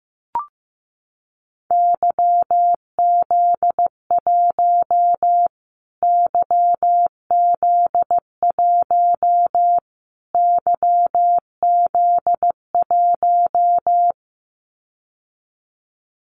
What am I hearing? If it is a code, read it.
Morse code, heard as a keyed tone, about 15 wpm, 700 Hz. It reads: YZ1 YZ1 YZ1
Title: Callsign Prefixes for Top 25 DXCC - Repeated 3x - 15wpm